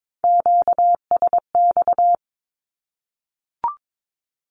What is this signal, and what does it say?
Morse code, heard as a keyed tone, about 22 wpm, 700 Hz. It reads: QSX
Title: Q-Codes Repeated 3x 22wpm